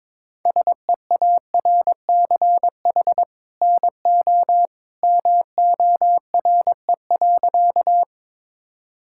Morse code, heard as a keyed tone, about 22 wpm, 700 Hz. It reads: SEARCH NO MORE.
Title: Sentences from Top 1000 Words - Review Part 5 - 22wpm